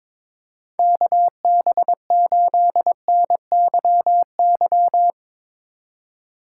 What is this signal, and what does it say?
Morse code, heard as a keyed tone, about 24 wpm, 700 Hz. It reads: KB8NYY
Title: Speed Racing - USA 2x3 Call Signs 22wpm